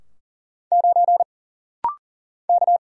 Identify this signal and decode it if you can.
Morse code, heard as a keyed tone, about 40 wpm, 700 Hz. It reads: 9 K
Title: Single Letter-Number - Rapid-Fire 40wpm